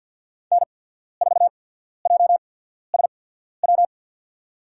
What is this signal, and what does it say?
Morse code, heard as a keyed tone, about 52 wpm, spaced as for 14 wpm, 700 Hz. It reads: N4JSW
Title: USA Call Signs - 5 Characters - 8x Character Spacing - 50wpm